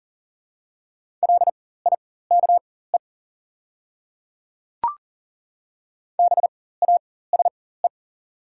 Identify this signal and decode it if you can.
Morse code, heard as a keyed tone, about 40 wpm, spaced as for 19 wpm, 700 Hz. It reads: LIKE BASE
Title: 4x Character Spacing - Top 300 Words - 40wpm